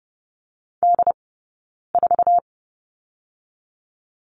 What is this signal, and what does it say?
Morse code, heard as a keyed tone, about 30 wpm, 700 Hz. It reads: D 4
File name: course / 30:10wpm / 086 Lesson 15d - Number 5 - Two Characters 30wpm